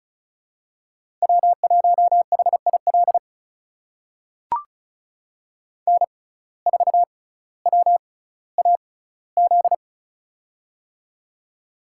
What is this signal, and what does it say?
Morse code, heard as a keyed tone, about 36 wpm, spaced as for 18 wpm, 700 Hz. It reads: W1HIL N4WAZ
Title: USA Call Signs - 5 Characters - 6x Character Spacing - 35wpm